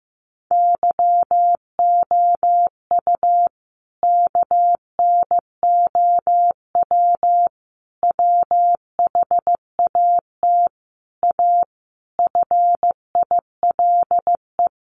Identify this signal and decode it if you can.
Morse code, heard as a keyed tone, about 15 wpm, 700 Hz. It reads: YOU KNOW WHAT A FILE
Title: Sentences from Top 1200 Words 15wpm